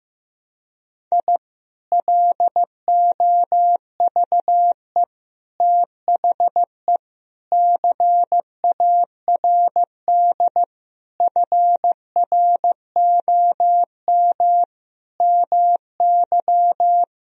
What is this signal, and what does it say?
Morse code, heard as a keyed tone, about 15 wpm, 700 Hz. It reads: I LOVE THE CARD FROM MY
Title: Sentences from Top 1000 Words - Review Part 4 - 15wpm